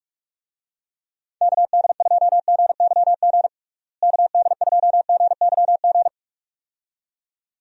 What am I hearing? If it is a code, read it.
Morse code, heard as a keyed tone, about 44 wpm, 700 Hz. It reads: KD2GYG KD2GYG
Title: USA Call Signs - 6 Characters - Repeated 2x - 45wpm